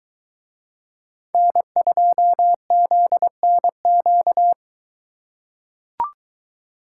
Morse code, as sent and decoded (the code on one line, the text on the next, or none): -. ..--- --.. -. --.-
N2ZNQ